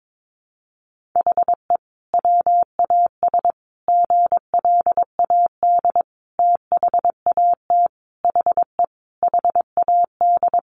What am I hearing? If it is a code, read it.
Morse code, heard as a keyed tone, about 22 wpm, 700 Hz. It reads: HE WAS GLAD THAT HE HAD
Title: Sentences from Top 1400 Words 22wpm